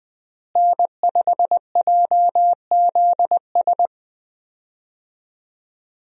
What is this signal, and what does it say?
Morse code, heard as a keyed tone, about 20 wpm, 700 Hz. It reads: N5JZS
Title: USA Call Signs - 5 Characters - Repeated 2x - 20wpm